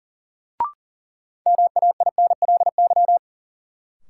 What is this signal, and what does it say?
Morse code, heard as a keyed tone, about 40 wpm, 700 Hz. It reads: MAINLY
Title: Sentences from Top 1400 Words - New Word Drill - 40wpm